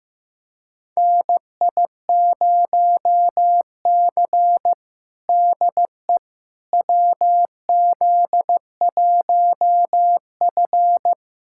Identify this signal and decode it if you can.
Morse code, heard as a keyed tone, about 15 wpm, 700 Hz. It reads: NI0C DE WZ1F
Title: USA Callsign DE Callsign - 4 Characters - 15wpm